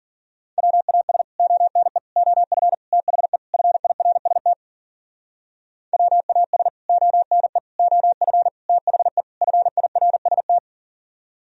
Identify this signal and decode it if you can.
Morse code, heard as a keyed tone, about 51 wpm, 700 Hz. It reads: WAS ONE OF THE FIRST WAS ONE OF THE FIRST